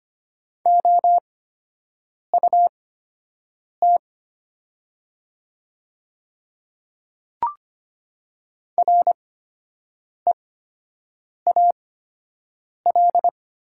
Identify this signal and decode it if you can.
Morse code, heard as a keyed tone, about 25 wpm, spaced as for 7 wpm, 700 Hz. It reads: OUT REAL
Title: 8x Character Spacing - Top 300 Words - 25wpm